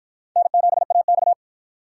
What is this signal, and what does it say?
Morse code, heard as a keyed tone, about 47 wpm, 700 Hz. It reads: N7AX